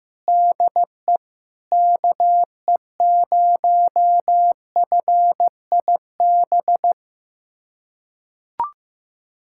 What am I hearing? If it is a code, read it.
Morse code, heard as a keyed tone, about 15 wpm, 700 Hz. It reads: DE KE0FIB
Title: USA Callsign DE Callsign - 6 Characters - 15wpm